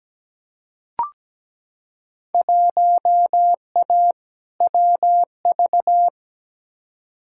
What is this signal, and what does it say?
Morse code, heard as a keyed tone, about 17 wpm, 700 Hz. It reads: 1A WV
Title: Field Day Exchanges Encore - 17wpm